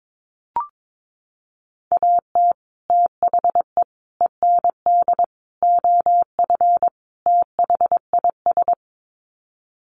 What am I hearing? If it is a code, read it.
Morse code, heard as a keyed tone, about 22 wpm, 700 Hz. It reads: AT THE END OF THIS
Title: Sets of 5 Words 22wpm